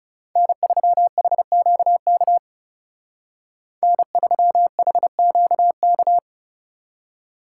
Morse code, heard as a keyed tone, about 39 wpm, 700 Hz. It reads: N3HQK N3HQK